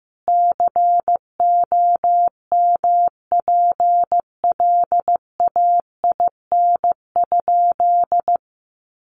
Morse code, heard as a keyed tone, about 14 wpm, 700 Hz. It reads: COMPLAIN?
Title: Sentences from Top 1300 Words 15wpm